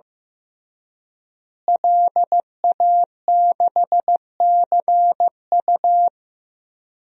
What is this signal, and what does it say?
Morse code, heard as a keyed tone, about 15 wpm, 700 Hz. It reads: LA6CU
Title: DXCC-100 Call Signs - 5 Characters - Repeated 3x - 15wpm